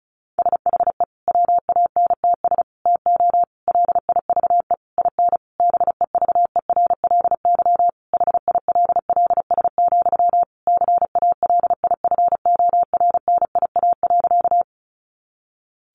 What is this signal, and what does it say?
Morse code, heard as a keyed tone, about 34 wpm, 700 Hz. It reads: SHE WANTS TO LIVE IN BEVERLY HILLS, CALIFORNIA.
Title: Easy Phrases 35wpm